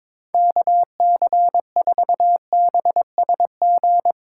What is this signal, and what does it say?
Morse code, heard as a keyed tone, about 22 wpm, 700 Hz. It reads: KC4BSG
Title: USA Call Signs - 6 Characters - 6x Character Spacing - 22wpm